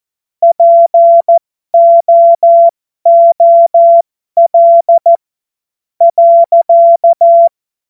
Morse code, heard as a keyed tone, about 14 wpm, spaced as for 10 wpm, 680 Hz. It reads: POOL.
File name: Morsekurs / Witze